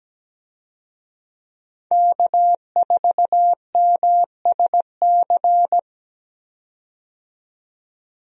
Morse code, heard as a keyed tone, about 17 wpm, 700 Hz. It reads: K4MSC
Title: USA Call Signs - 5 Characters - 17wpm